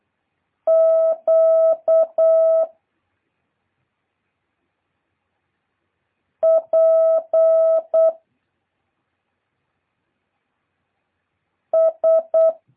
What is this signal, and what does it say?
Morse code, heard as a keyed tone, about 8 wpm, 650 Hz. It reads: Q P S
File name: jh8cbh / 2023cw